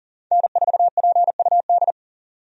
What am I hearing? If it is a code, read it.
Morse code, heard as a keyed tone, about 40 wpm, 700 Hz. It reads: N4PUD
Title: USA Call Signs - 5 Characters - 40wpm